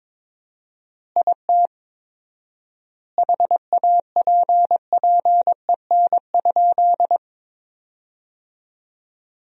Morse code, heard as a keyed tone, about 22 wpm, 700 Hz. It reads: IT HAPPEN?
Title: 4x Word-Spacing - Sentences from Top 400 Words - 22wpm